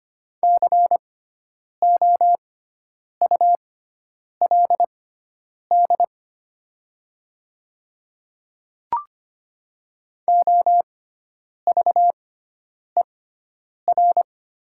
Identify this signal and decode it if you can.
Morse code, heard as a keyed tone, about 25 wpm, spaced as for 9 wpm, 690 Hz. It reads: COULD OVER
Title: 6x Character Spacing - Top 200 Words - 25wpm